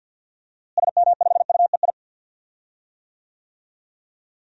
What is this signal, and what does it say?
Morse code, heard as a keyed tone, about 53 wpm, 700 Hz. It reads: SM5UEI
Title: DXCC-100 Call Signs - 6 Characters - Repeated 2x - 50wpm